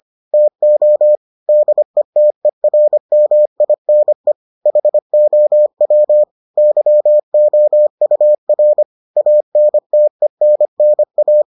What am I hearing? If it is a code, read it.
Morse code, heard as a keyed tone, about 25 wpm, 600 Hz. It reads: TO DETERMINE HOW YOUR ANTENNA